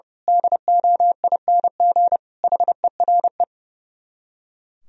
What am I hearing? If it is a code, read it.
Morse code, heard as a keyed tone, about 28 wpm, 700 Hz. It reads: DOING HERE
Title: Speed-Racing - Sets of 5 Words 30wpm